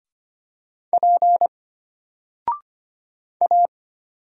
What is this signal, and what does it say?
Morse code, heard as a keyed tone, about 25 wpm, 700 Hz. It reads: P A
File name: Single-Letter-Number-25wpm